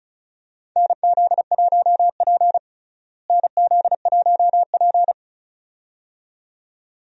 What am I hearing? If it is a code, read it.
Morse code, heard as a keyed tone, about 34 wpm, 700 Hz. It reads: NZ1P NZ1P
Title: USA Call Signs - 4 Characters - Repeated 2x - 35wpm